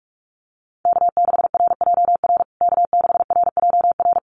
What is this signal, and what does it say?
Morse code, heard as a keyed tone, about 44 wpm, 700 Hz. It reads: K6RJR K6RJR
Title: USA Call Signs Repeated 2x 45wpm